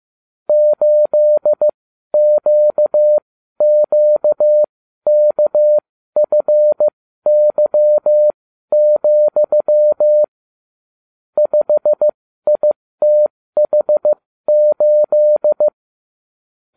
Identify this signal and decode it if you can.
Morse code, heard as a keyed tone, about 15 wpm, spaced as for 12 wpm, 600 Hz. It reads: 8QQKFY, 5ITH8